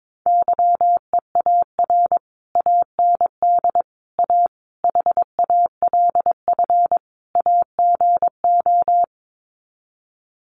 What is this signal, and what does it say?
Morse code, heard as a keyed tone, about 22 wpm, 700 Hz. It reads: YEAR AND A HALF AGO
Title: Sets of 5 Words Encore 22wpm